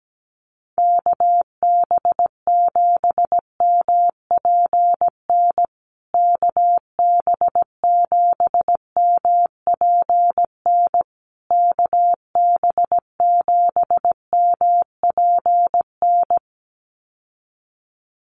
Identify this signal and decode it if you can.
Morse code, heard as a keyed tone, about 17 wpm, 700 Hz. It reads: KB7MPN KB7MPN KB7MPN
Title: USA Call Signs Repeated 3x 17wpm